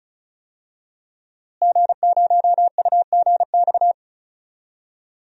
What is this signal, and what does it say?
Morse code, heard as a keyed tone, about 34 wpm, 700 Hz. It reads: G0UGX
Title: DXCC-100 Call Signs - 5 Characters - Repeated 2x - 35wpm